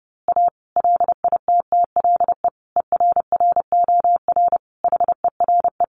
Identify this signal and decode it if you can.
Morse code, heard as a keyed tone, about 30 wpm, 700 Hz. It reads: A LITTLE ERROR HERE
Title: A Sound Of Thunder 30wpm